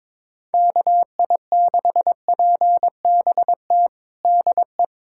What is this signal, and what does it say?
Morse code, heard as a keyed tone, about 22 wpm, 700 Hz. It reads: KI6PBT DE
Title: USA Callsign DE Callsign - 6 Characters - 22wpm